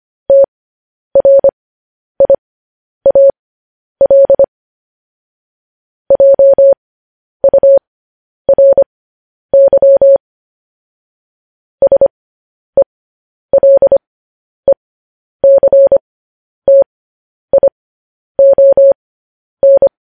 Morse code, heard as a keyed tone, about 25 wpm, spaced as for 10 wpm, 550 Hz. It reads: TRIAL JURY SELECTION
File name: podcast / News headlines-1x-25x10